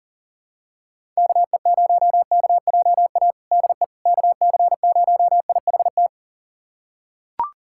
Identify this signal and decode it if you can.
Morse code, heard as a keyed tone, about 37 wpm, 700 Hz. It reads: KE0KJA DE KC0IHT
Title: USA Callsign DE Callsign - 6 Characters - 40wpm